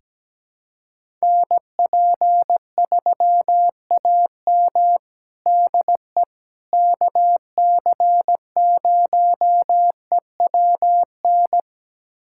Morse code, heard as a keyed tone, about 17 wpm, 700 Hz. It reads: NP3AM DE KC0EWN
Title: USA Callsign DE Callsign 17wpm